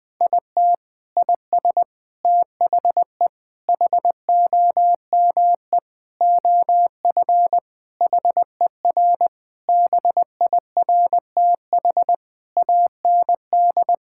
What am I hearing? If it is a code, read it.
Morse code, heard as a keyed tone, about 21 wpm, 700 Hz. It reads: IT IS THE HOME OF HER BIRTH AND
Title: Sentences from Top 1000 Words - Review Part 5 - 20wpm